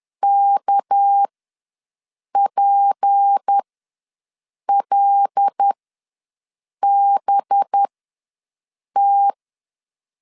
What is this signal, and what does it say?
Morse code, heard as a keyed tone, about 10 wpm, spaced as for 6 wpm, 800 Hz. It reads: KPLBT
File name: LEZIONI / 6 PKX0 / 6 PKX0 ws 10wpm 4220